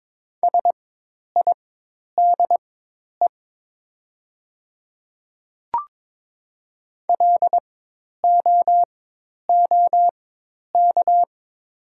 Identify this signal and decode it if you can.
Morse code, heard as a keyed tone, about 22 wpm, spaced as for 10 wpm, 700 Hz. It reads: SIDE LOOK
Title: 4x Character Spacing - Top 100 Words - 22wpm